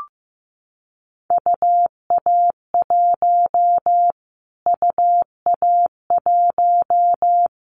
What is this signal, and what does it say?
Morse code, heard as a keyed tone, about 15 wpm, 700 Hz. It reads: UA1 UA1
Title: Callsign Prefixes for Top 25 DXCC - Repeated 3x - 15wpm